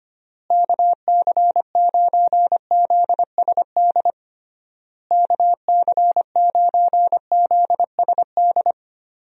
Morse code, heard as a keyed tone, about 25 wpm, 700 Hz. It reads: KC9ZSD KC9ZSD
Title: USA Call Signs - 6 Characters - Repeated 3x - 25wpm